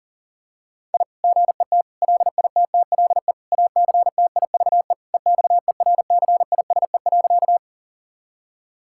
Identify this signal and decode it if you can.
Morse code, heard as a keyed tone, about 40 wpm, 700 Hz. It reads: I GET LITTLE ACTIVE EXERCISE.
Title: Sentences from Top 1000 Words - Review Part 6 - 40wpm